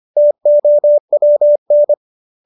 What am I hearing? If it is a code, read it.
Morse code, heard as a keyed tone, about 25 wpm, 590 Hz. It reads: TOWN